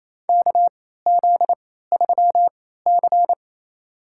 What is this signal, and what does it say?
Morse code, heard as a keyed tone, about 28 wpm, spaced as for 16 wpm, 700 Hz. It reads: KZ3C
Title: USA Call Signs - 3x Character Spacing - 28wpm